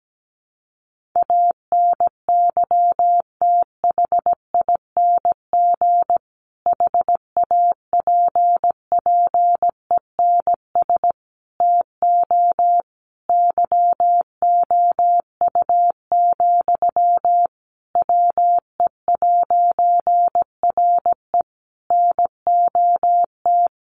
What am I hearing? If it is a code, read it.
Morse code, heard as a keyed tone, about 17 wpm, 700 Hz. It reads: ANYTHING HAPPENS TO YOU, WE'RE NOT
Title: A Sound Of Thunder 17wpm